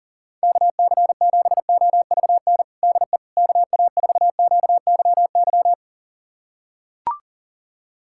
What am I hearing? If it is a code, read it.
Morse code, heard as a keyed tone, about 40 wpm, 700 Hz. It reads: KC7OVN DE KA4QYY
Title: USA Callsign DE Callsign 40wpm